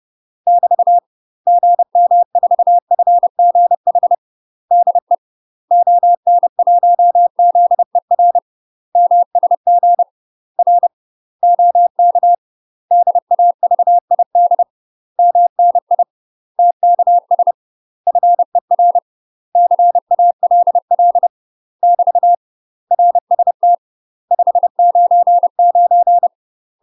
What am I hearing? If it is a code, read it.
Morse code, heard as a keyed tone, about 30 wpm, 700 Hz. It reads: <BT> GM4FGH DE ON1ZER MSG R OK DAVID MNI TKS FER CALL <BT> RST 599
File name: UK MASTER QSOs TRAINING / 30WPM / UK MASTER QSOs 0029